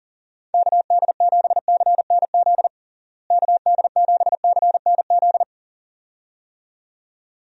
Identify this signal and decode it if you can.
Morse code, heard as a keyed tone, about 37 wpm, 700 Hz. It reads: KD7CNZ KD7CNZ
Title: USA Call Signs - 6 Characters - Repeated 2x - 40wpm